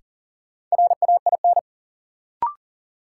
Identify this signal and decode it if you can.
Morse code, heard as a keyed tone, about 40 wpm, 700 Hz. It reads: RAIN
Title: Sentences from Top 600 Words - New Word Drill - 40wpm